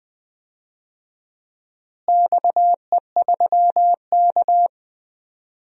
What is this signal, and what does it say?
Morse code, heard as a keyed tone, about 21 wpm, 700 Hz. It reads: XE3K
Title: DXCC-100 Call Signs - 4 Characters - Repeated 2x - 20wpm